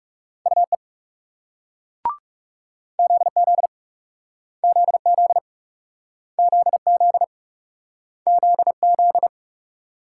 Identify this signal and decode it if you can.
Morse code, heard as a keyed tone, about 39 wpm, 700 Hz. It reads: UE ZZ ZZ ZZ ZZ